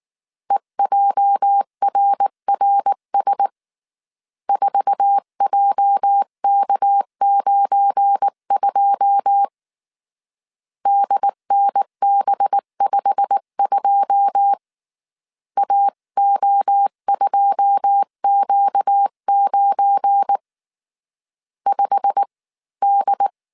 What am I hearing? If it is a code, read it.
Morse code, heard as a keyed tone, about 19 wpm, 800 Hz. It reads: EJRRS 4JK92 DNB52 AO2Q9 5D
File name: CW / DET / 1 9 1500C S C 18wpm 2539